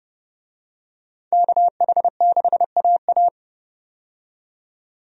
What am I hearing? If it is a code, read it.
Morse code, heard as a keyed tone, about 32 wpm, 700 Hz. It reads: KH6AA
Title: DXCC-100 Call Signs - 5 Characters - 30wpm